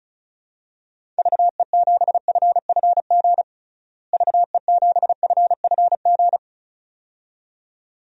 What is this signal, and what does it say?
Morse code, heard as a keyed tone, about 36 wpm, 700 Hz. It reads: VE7FFG VE7FFG
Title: DXCC-100 Call Signs - 6 Characters - Repeated 2x - 35wpm